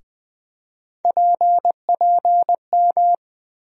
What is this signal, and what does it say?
Morse code, heard as a keyed tone, about 21 wpm, 700 Hz. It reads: PPM
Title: USA Call Sign Postfixes - 3 Letters 20wpm